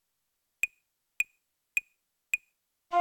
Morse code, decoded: none